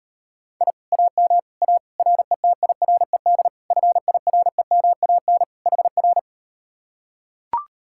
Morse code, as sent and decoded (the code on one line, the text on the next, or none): .. / .- -- / .- / .-. - .. .-. -.. / ..-. .. .-. -- .- -. / .... .-.
I AM A RETIRED FIREMAN HR